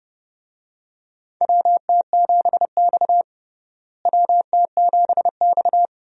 Morse code, heard as a keyed tone, about 30 wpm, 700 Hz. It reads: WT7X WT7X
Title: USA Call Signs - 4 Characters - Repeated 3x - 30wpm